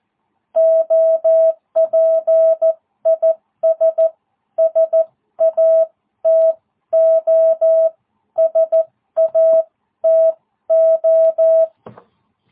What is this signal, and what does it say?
Morse code, heard as a keyed tone, about 14 wpm, spaced as for 12 wpm, 660 Hz. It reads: OP IS SATO SATO